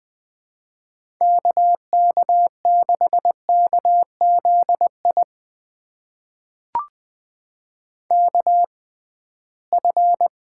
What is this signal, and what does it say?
Morse code, heard as a keyed tone, about 20 wpm, spaced as for 15 wpm, 700 Hz. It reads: KK6KZI KF
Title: USA Call Signs - 6 Characters - 6x Character Spacing - 20wpm